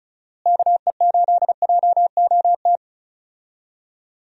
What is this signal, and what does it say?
Morse code, heard as a keyed tone, about 36 wpm, 700 Hz. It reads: KE8JOT
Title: USA Call Signs - 6 Characters - Repeated 3x - 35wpm